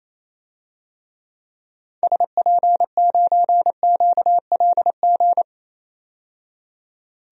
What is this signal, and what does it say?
Morse code, heard as a keyed tone, about 28 wpm, 700 Hz. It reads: SP9QLG
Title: DXCC-100 Call Signs - 6 Characters - 28wpm